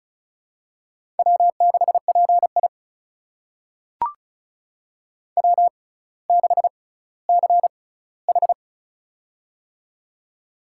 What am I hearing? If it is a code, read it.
Morse code, heard as a keyed tone, about 36 wpm, spaced as for 18 wpm, 700 Hz. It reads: W6PI W6CH
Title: USA Call Signs - 4 Characters - 6x Character Spacing - 35wpm